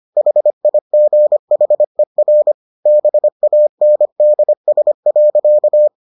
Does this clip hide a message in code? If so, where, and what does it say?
Yes: Morse code from 0.2 to 5.9 s, heard as a keyed tone, about 25 wpm, 600 Hz. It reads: HIGHER BANDS.